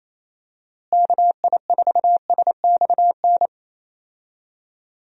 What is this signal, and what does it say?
Morse code, heard as a keyed tone, about 27 wpm, 700 Hz. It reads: KI4SXN
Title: USA Call Signs - 6 Characters - Repeated 3x - 28wpm